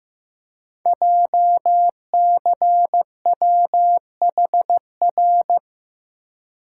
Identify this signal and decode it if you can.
Morse code, heard as a keyed tone, about 15 wpm, 700 Hz. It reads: JCWHR
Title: Code Groups - 5 Letters - 15wpm